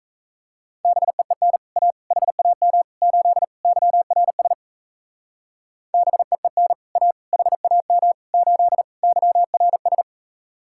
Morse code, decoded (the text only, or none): BEEN A HAM 8 YRS BEEN A HAM 8 YRS